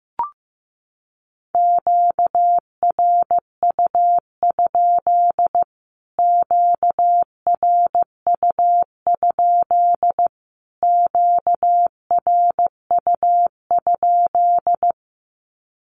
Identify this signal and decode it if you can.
Morse code, heard as a keyed tone, about 15 wpm, 700 Hz. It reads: QRU? QRU? QRU?
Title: Q-Codes Repeated 3x 15wpm